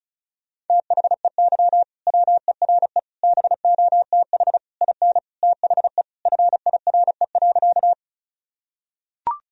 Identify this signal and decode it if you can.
Morse code, heard as a keyed tone, about 34 wpm, 700 Hz. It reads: THEY WERE BOTH IN THE FIRE.
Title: Sentences from Top 1000 Words - Review Part 1 - 35wpm